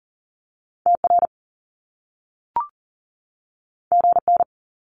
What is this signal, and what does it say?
Morse code, heard as a keyed tone, about 40 wpm, 700 Hz. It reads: TR GN
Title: Top Two Letter Combinations in Words 40wpm